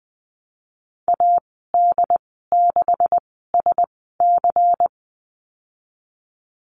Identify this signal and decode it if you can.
Morse code, heard as a keyed tone, about 20 wpm, 700 Hz. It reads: A D 6 S C
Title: 154 Lesson 25g - Letter p - Call Signs 20wpm